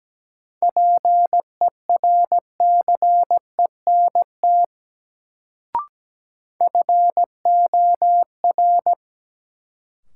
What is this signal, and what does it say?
Morse code, heard as a keyed tone, about 17 wpm, 700 Hz. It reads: PERCENT FOR
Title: Sets of 2 Words - New Word Drill - 17wpm